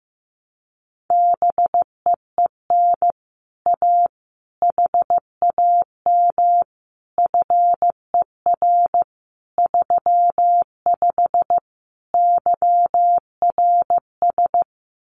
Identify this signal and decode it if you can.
Morse code, heard as a keyed tone, about 15 wpm, 700 Hz. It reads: BEEN A HAM FER 35 YRS
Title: Common QSO Phrases Encore 15wpm